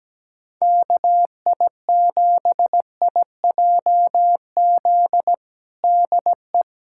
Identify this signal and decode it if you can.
Morse code, heard as a keyed tone, about 17 wpm, 700 Hz. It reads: KI7IJZ DE
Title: USA Callsign DE Callsign 17wpm